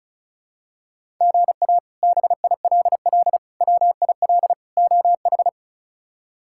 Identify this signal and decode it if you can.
Morse code, heard as a keyed tone, about 36 wpm, 700 Hz. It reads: GA BILL WIL OH
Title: K1USN SST QSO Practice - 35wpm